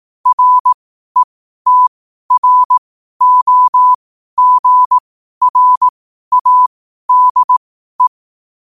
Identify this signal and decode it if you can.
Morse code, heard as a keyed tone, about 18 wpm, spaced as for 12 wpm, 1000 Hz. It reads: RETROGRADE
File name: qotdfiles / 12wpm